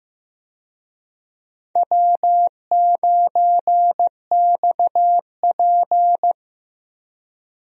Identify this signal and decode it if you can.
Morse code, heard as a keyed tone, about 16 wpm, 700 Hz. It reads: W9XP
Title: USA Call Signs - 4 Characters - Repeated 3x - 15wpm